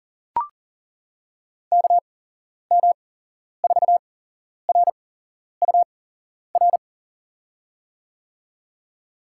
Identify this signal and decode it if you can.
Morse code, heard as a keyed tone, about 40 wpm, spaced as for 11 wpm, 700 Hz. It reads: KM4RUR
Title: USA Call Signs - 6 Characters - 8x Character Spacing - 40wpm